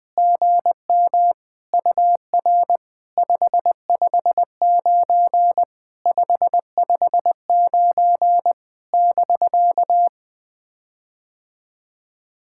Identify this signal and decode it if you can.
Morse code, heard as a keyed tone, about 20 wpm, 700 Hz. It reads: GM UR 559 559 <BK>
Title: POTA QSO Practice - 20wpm